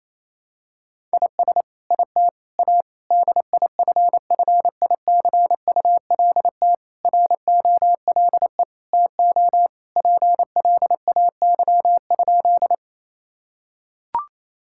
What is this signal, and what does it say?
Morse code, heard as a keyed tone, about 28 wpm, 700 Hz. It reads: IS IT A DIFFICULT ROLE TO PLAY?
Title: Sentences from Top 1000 Words - Review Part 6 - 28wpm